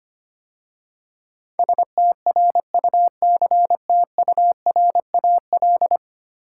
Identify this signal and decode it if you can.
Morse code, heard as a keyed tone, about 25 wpm, 700 Hz. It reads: STRUCTURAL
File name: 10 Letter Words - First 3 Letters Given - 25wpm